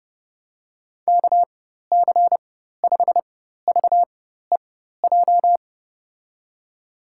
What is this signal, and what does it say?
Morse code, heard as a keyed tone, about 30 wpm, spaced as for 14 wpm, 700 Hz. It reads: KC5VEJ
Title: USA Call Signs - 4x Character Spacing - 30wpm